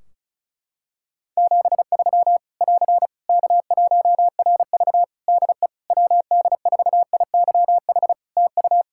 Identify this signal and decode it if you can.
Morse code, heard as a keyed tone, about 36 wpm, 700 Hz. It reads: 73 <AR> K1RV DE WD4IYH TU
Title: Practice QSO - Exchange 4 - Ending - 35wpm